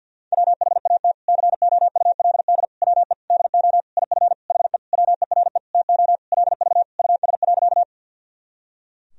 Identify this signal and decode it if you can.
Morse code, heard as a keyed tone, about 52 wpm, 700 Hz. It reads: WHAT COULD WE DO IF HE WERE TO LV US.
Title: Sentences from Top 400 Words and Abbreviations 50wpm